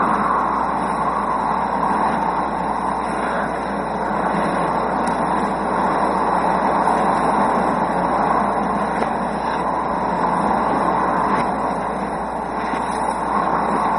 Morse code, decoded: none

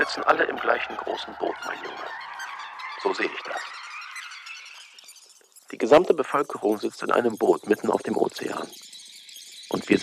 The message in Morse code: none

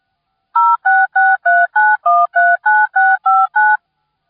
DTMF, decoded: 06639139659

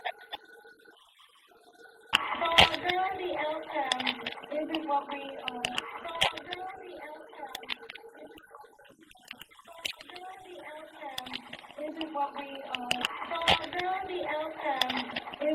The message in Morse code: none

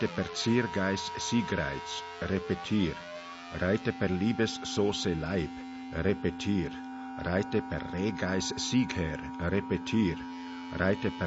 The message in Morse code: none

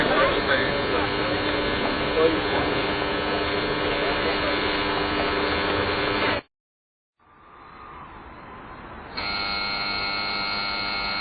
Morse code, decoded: none

